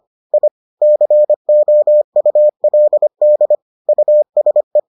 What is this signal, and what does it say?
Morse code, heard as a keyed tone, about 25 wpm, 600 Hz. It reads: I COULD USE